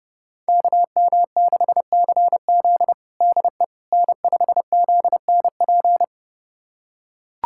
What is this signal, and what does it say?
Morse code, heard as a keyed tone, about 31 wpm, 700 Hz. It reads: KM6CZ DE N5ZNP
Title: USA Callsign DE Callsign - 5 Characters - 30wpm